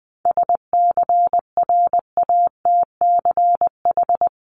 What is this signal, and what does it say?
Morse code, heard as a keyed tone, about 20 wpm, 700 Hz. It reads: SCRATCH
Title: Top 1501-1750 Words 20wpm